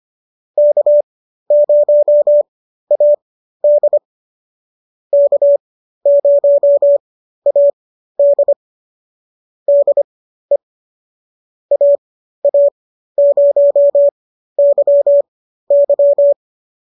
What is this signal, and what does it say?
Morse code, heard as a keyed tone, about 25 wpm, spaced as for 13 wpm, 590 Hz. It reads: K0AD K0AD DE AA0YY